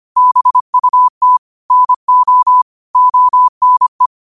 Morse code, heard as a keyed tone, about 25 wpm, 1000 Hz. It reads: BUT NO ONE
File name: qotdfiles / 25wpm